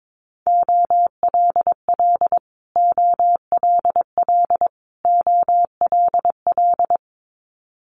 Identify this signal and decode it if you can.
Morse code, heard as a keyed tone, about 22 wpm, 700 Hz. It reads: OLL OLL OLL
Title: Multi-letter Phonemes Repeated 3x 22wpm